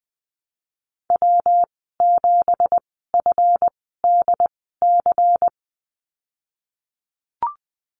Morse code, heard as a keyed tone, about 20 wpm, spaced as for 14 wpm, 700 Hz. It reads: W7FDC
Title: 193 Lesson 31g - Letter v - Call Signs 20wpm